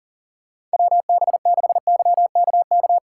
Morse code, heard as a keyed tone, about 40 wpm, 700 Hz. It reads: WB6YKK
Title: USA Call Signs - 6 Characters - 2x Character Spacing - 40wpm